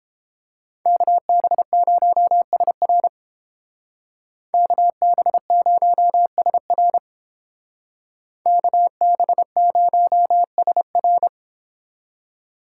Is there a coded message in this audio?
Yes: Morse code from 0.9 to 11.3 s, heard as a keyed tone, about 31 wpm, 700 Hz. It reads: KB0SR KB0SR KB0SR